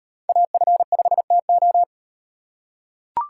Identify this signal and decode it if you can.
Morse code, heard as a keyed tone, about 38 wpm, 700 Hz. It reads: AF5TO